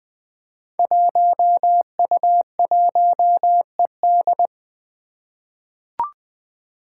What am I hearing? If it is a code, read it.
Morse code, heard as a keyed tone, about 21 wpm, 700 Hz. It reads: 1U1ED
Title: Five Challenging Letter-Number Combinations 20wpm